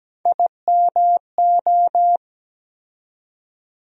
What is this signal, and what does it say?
Morse code, heard as a keyed tone, about 17 wpm, 700 Hz. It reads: IMO
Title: USA Call Sign Postfixes - 3 Letters 17wpm